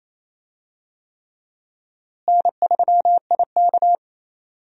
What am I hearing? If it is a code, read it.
Morse code, heard as a keyed tone, about 27 wpm, 700 Hz. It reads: N3IK
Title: USA Call Signs - 4 Characters - Repeated 3x - 28wpm